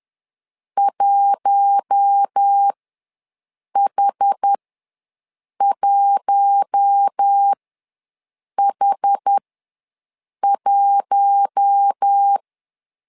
Morse code, heard as a keyed tone, about 10 wpm, spaced as for 6 wpm, 800 Hz. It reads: 1H1H1
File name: LEZIONI / 4 Q1 ws 10wpm 2538